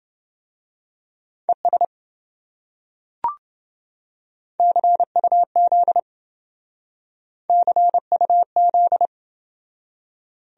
Morse code, heard as a keyed tone, about 27 wpm, 700 Hz. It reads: ES CUZ CUZ